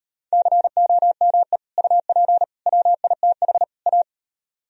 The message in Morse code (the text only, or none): COME UP WITH A